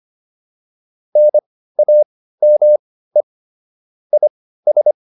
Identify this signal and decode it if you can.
Morse code, heard as a keyed tone, about 25 wpm, spaced as for 15 wpm, 600 Hz. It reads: NAME IS